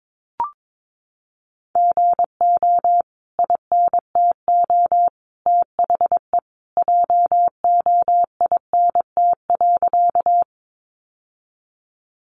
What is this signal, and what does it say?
Morse code, heard as a keyed tone, about 22 wpm, 700 Hz. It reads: GO INTO THE JOINT.
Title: Sentences from Top 1300 Words 22wpm